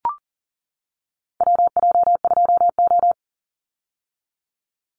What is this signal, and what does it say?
Morse code, heard as a keyed tone, about 40 wpm, 700 Hz. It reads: WJ2O
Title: Call Signs - 40wpm